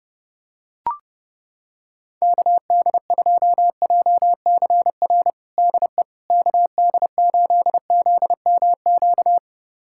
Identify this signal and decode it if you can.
Morse code, heard as a keyed tone, about 30 wpm, 700 Hz. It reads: KD2JCR DE KD8ZMQ
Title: USA Callsign DE Callsign 30wpm